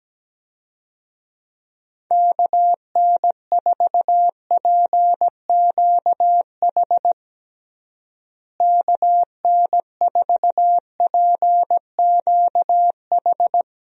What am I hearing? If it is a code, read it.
Morse code, heard as a keyed tone, about 17 wpm, 700 Hz. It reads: KN4PQH KN4PQH
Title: USA Call Signs - 6 Characters - Repeated 2x - 17wpm